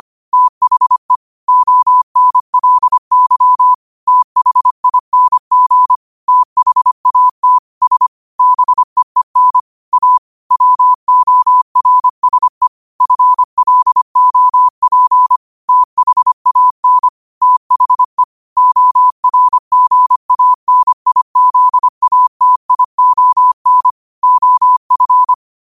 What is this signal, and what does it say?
Morse code, heard as a keyed tone, about 25 wpm, 1000 Hz. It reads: THE ONLY THING THAT S BEEN A WORSE FLOP THAN THE ORGANIZATION OF
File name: qotdfiles / 25wpm